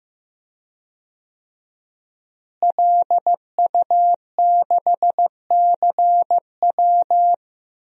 Morse code, heard as a keyed tone, about 15 wpm, 700 Hz. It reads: LU6CW
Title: DXCC-100 Call Signs - 5 Characters - Repeated 2x - 15wpm